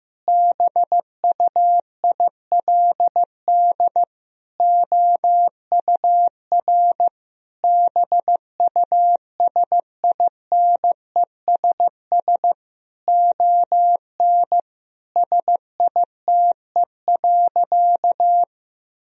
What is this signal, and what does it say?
Morse code, heard as a keyed tone, about 16 wpm, 700 Hz. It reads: BUILD OUR BUSINESS ON SITE.
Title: Sentences from Top 1000 Words - Review Part 4 - 15wpm